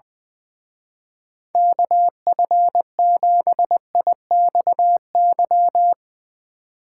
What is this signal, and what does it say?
Morse code, heard as a keyed tone, about 22 wpm, 700 Hz. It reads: KF7IXY
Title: Speed Racing - USA Call Signs 20wpm